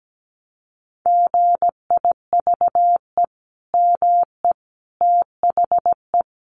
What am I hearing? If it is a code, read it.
Morse code, heard as a keyed tone, about 17 wpm, 700 Hz. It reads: GIVE ME THE
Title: Sentences from Top 1100 Words 17wpm